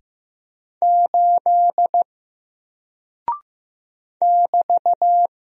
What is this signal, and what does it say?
Morse code, heard as a keyed tone, about 15 wpm, 700 Hz. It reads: 8 <BT>
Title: Single Character (Letter:Number:Punctuation:Prosign) Alt 15wpm